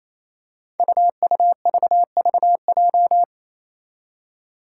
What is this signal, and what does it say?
Morse code, heard as a keyed tone, about 28 wpm, 700 Hz. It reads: UUVVJ
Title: Five Challenging Letter-Number Combinations 28wpm